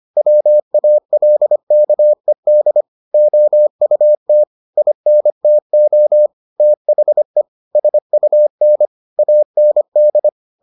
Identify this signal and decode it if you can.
Morse code, heard as a keyed tone, about 25 wpm, 600 Hz. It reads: WALKED OUT INTO THE SUN AND